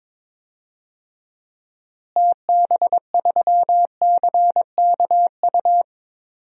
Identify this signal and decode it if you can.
Morse code, heard as a keyed tone, about 22 wpm, 700 Hz. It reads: TB3CKU
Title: DXCC-100 Call Signs - 6 Characters - Repeated 2x - 22wpm